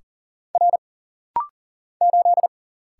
Morse code, heard as a keyed tone, about 40 wpm, 700 Hz. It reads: R 8
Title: Single Letter-Number - Rapid-Fire 40wpm